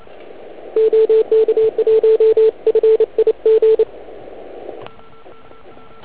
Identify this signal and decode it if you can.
Morse code, heard as a keyed tone, about 29 wpm, 440 Hz. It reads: OK1FIG